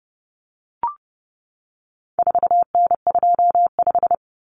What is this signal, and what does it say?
Morse code, heard as a keyed tone, about 30 wpm, 700 Hz. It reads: 4N25
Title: Common Cut Numbers - 4-Digit 30wpm